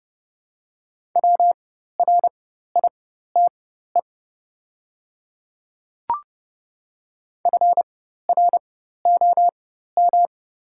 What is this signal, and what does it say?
Morse code, heard as a keyed tone, about 30 wpm, spaced as for 14 wpm, 700 Hz. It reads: WRITE FROM
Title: 4x Character Spacing - Top 100 Words - 30wpm